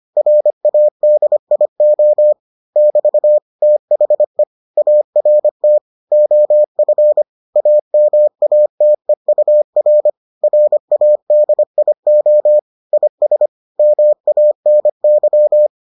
Morse code, heard as a keyed tone, about 25 wpm, 600 Hz. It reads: RADIO <BT> THE ART OF AMATEUR RADIO IS MANY